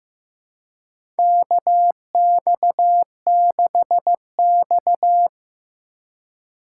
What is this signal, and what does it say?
Morse code, heard as a keyed tone, about 15 wpm, 700 Hz. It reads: KX6X
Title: K1USN SST QSO Practice - 15wpm